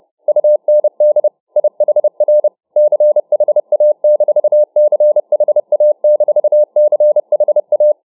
Morse code, heard as a keyed tone, about 30 wpm, 600 Hz. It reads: UND IHR CHA-CHA-CHA